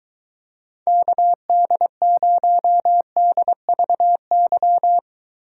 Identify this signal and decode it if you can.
Morse code, heard as a keyed tone, about 23 wpm, 700 Hz. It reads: KD0DVY